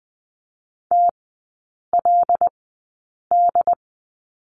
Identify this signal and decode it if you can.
Morse code, heard as a keyed tone, about 20 wpm, spaced as for 8 wpm, 700 Hz. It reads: TLD
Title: 107 Lesson 18e - Letter m - Three Characters 20wpm